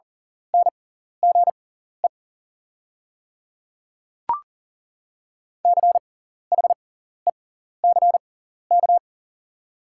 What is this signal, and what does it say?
Morse code, heard as a keyed tone, about 40 wpm, spaced as for 14 wpm, 690 Hz. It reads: NGE CHECK
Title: 6x Character Spacing - Top 500 Words - 40wpm